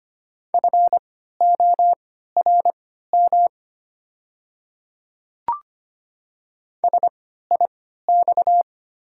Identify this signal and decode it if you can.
Morse code, heard as a keyed tone, about 25 wpm, spaced as for 14 wpm, 690 Hz. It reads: FORM SIX
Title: 3x Character Spacing - Top 500 Words - 25wpm